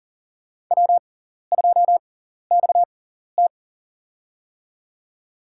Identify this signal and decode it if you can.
Morse code, heard as a keyed tone, about 40 wpm, spaced as for 14 wpm, 700 Hz. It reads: W2XT
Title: USA Call Signs - 4 Characters - 6x Character Spacing - 40wpm